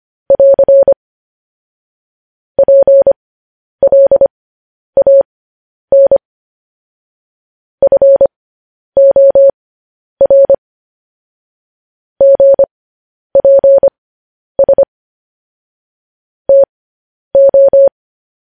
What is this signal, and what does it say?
Morse code, heard as a keyed tone, about 25 wpm, spaced as for 10 wpm, 550 Hz. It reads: <AR> PLAN FOR GPS TO
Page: news headlines in morse code at 10 wpm